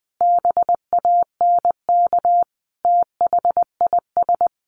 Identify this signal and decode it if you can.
Morse code, heard as a keyed tone, about 19 wpm, 700 Hz. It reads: BANK THIS
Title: Sentences from Top 900 Words 20wpm